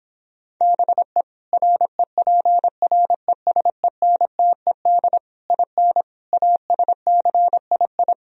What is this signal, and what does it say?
Morse code, heard as a keyed tone, about 26 wpm, 700 Hz. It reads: BE REPRESENTED IN ASCII